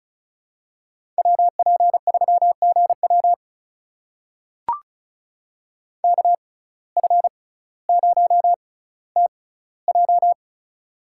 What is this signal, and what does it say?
Morse code, heard as a keyed tone, about 34 wpm, spaced as for 18 wpm, 700 Hz. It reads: WP3GW KF0TJ
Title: USA Call Signs - 5 Characters - 6x Character Spacing - 35wpm